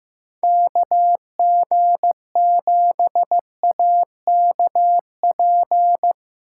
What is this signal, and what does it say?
Morse code, heard as a keyed tone, about 15 wpm, 700 Hz. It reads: KG7AKP
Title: USA Call Signs - 6 Characters - 4x Character Spacing - 15wpm